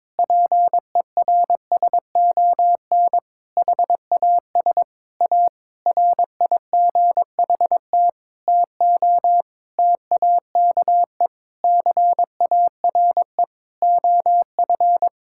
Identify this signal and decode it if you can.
Morse code, heard as a keyed tone, about 22 wpm, 700 Hz. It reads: PERSON HAS A RIGHT TO TAKE CARE OF
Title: Sentences from Top 1000 Words - Review Part 5 - 22wpm